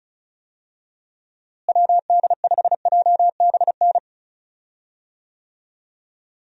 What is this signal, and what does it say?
Morse code, heard as a keyed tone, about 36 wpm, 700 Hz. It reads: WD5JBN
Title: USA Call Signs - 6 Characters - Repeated 2x - 35wpm